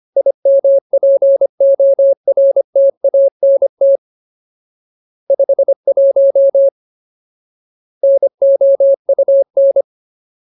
Morse code, heard as a keyed tone, about 25 wpm, 550 Hz. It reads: IMPORTANT 51 NOUN
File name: Ultimate Word List / 25WPM Ultimate